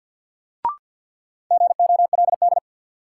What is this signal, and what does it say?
Morse code, heard as a keyed tone, about 46 wpm, 700 Hz. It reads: GOLD
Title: Sentences from Top 1100 Words - New Word Drill Alt - 50wpm